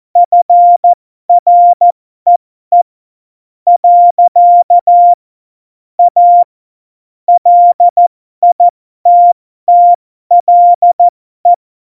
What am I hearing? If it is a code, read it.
Morse code, heard as a keyed tone, about 14 wpm, spaced as for 11 wpm, 700 Hz. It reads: FREE. A LITTLE
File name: Morsekurs / Witze